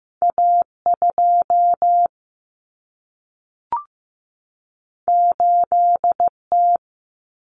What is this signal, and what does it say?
Morse code, heard as a keyed tone, about 15 wpm, 700 Hz. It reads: A2 8T
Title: Common Cut Numbers - 2-Digit 15wpm